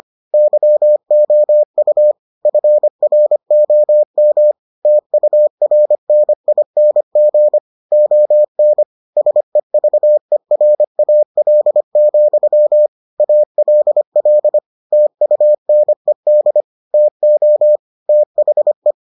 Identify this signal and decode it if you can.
Morse code, heard as a keyed tone, about 25 wpm, 600 Hz. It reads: YOU FROM TURNING ON SEVERAL, ALL TUNED TO THE